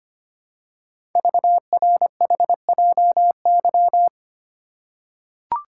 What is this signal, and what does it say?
Morse code, heard as a keyed tone, about 25 wpm, 700 Hz. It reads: VRHJY
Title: Code Groups - 5 Letter:Numbers - 25wpm